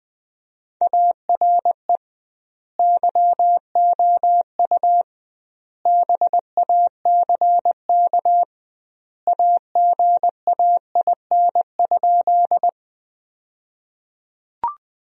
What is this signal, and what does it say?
Morse code, heard as a keyed tone, about 20 wpm, 700 Hz. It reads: ARE YOU BACK AGAIN?
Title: 2x Word-Spacing - Sentences from Top 200 Words - 20wpm